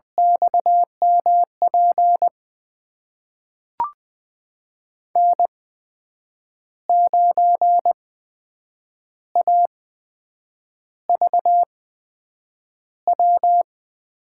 Morse code, heard as a keyed tone, about 20 wpm, spaced as for 7 wpm, 700 Hz. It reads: XMP N9AVW
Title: USA Call Signs - 5 Characters - 8x Character Spacing - 20wpm